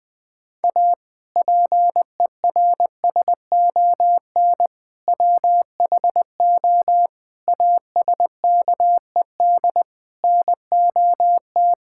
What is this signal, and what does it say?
Morse code, heard as a keyed tone, about 21 wpm, 700 Hz. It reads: A PERSON WHO ASKED NOT
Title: Sentences with 6-Word Phrases - Spoken Beginning - 20wpm